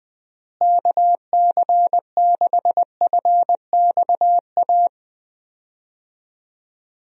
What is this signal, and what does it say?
Morse code, heard as a keyed tone, about 20 wpm, 700 Hz. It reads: KC6FXA
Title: USA Call Signs - 6 Characters - Repeated 2x - 20wpm